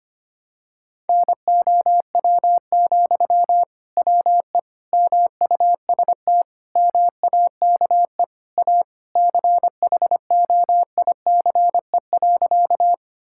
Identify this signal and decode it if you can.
Morse code, heard as a keyed tone, about 25 wpm, 700 Hz. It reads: NOW, WE MUST MAKE A CHOICE.
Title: Sentences from Top 1000 Words - Review Part 6 - 25wpm